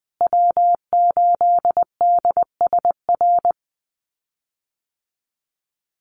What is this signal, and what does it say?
Morse code, heard as a keyed tone, about 19 wpm, 700 Hz. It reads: W8DSR
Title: USA Call Signs Repeated 3x 20wpm